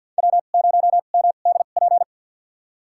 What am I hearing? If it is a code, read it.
Morse code, heard as a keyed tone, about 52 wpm, spaced as for 37 wpm, 700 Hz. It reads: W0MDP
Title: USA Call Signs - 5 Characters - 2x Character Spacing - 50wpm